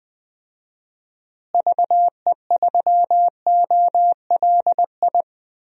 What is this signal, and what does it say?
Morse code, heard as a keyed tone, about 21 wpm, 700 Hz. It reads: VE3OLI
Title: DXCC-100 Call Signs - 6 Characters - Repeated 2x - 20wpm